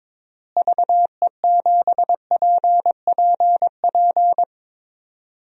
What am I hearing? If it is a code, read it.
Morse code, heard as a keyed tone, about 22 wpm, 700 Hz. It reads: VE7PPP
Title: DXCC-100 Call Signs - 6 Characters - Repeated 3x - 22wpm